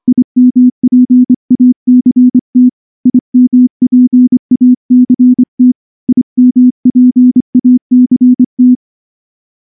Morse code, heard as a keyed tone, about 23 wpm, 250 Hz. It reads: IMPACT IMPACT IMPACT
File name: morse / morse-training-25WPM